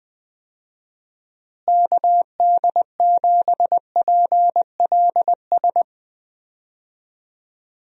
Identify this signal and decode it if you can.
Morse code, heard as a keyed tone, about 20 wpm, 700 Hz. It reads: KD7PLS
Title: USA Call Signs - 6 Characters - Repeated 2x - 20wpm